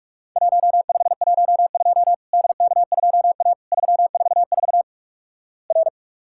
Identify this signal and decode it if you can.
Morse code, heard as a keyed tone, about 46 wpm, 700 Hz. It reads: 1512 DK2A 344 R
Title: Worked All Europe DX Contest - QTC Exchange - 45wpm